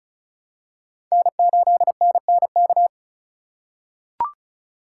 Morse code, heard as a keyed tone, about 34 wpm, 700 Hz. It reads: N8NNK
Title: USA Call Signs - 5 Characters - 4x Character Spacing - 35wpm